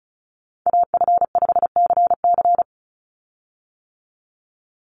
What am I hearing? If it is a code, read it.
Morse code, heard as a keyed tone, about 34 wpm, 700 Hz. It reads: AF5CC
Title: Call Signs - 35wpm